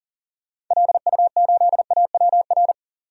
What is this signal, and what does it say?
Morse code, heard as a keyed tone, about 40 wpm, 700 Hz. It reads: LU8AWR
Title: DXCC-100 Call Signs - 6 Characters - 40wpm